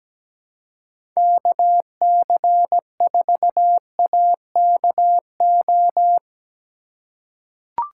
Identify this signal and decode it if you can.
Morse code, heard as a keyed tone, about 17 wpm, 700 Hz. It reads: KC4AKO
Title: USA Call Signs - 6 Characters - 3x Character Spacing - 17wpm